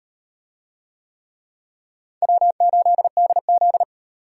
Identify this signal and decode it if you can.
Morse code, heard as a keyed tone, about 38 wpm, 700 Hz. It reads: W8DZ